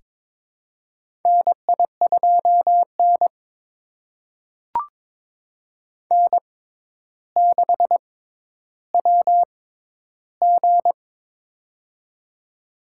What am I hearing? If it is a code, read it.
Morse code, heard as a keyed tone, about 22 wpm, spaced as for 11 wpm, 700 Hz. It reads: NI2N N6WG
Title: USA Call Signs - 4 Characters - 6x Character Spacing - 22wpm